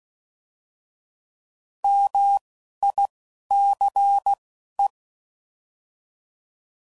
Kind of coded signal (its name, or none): Morse code